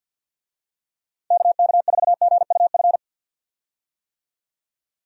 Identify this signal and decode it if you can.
Morse code, heard as a keyed tone, about 50 wpm, 700 Hz. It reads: KK4GUF
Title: USA Call Signs - 6 Characters - 50wpm